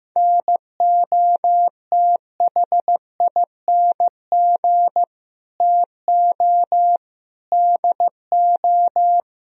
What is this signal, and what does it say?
Morse code, heard as a keyed tone, about 16 wpm, 700 Hz. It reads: NOTHING TO DO
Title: Speed-Racing - Sets of 5 Words 15wpm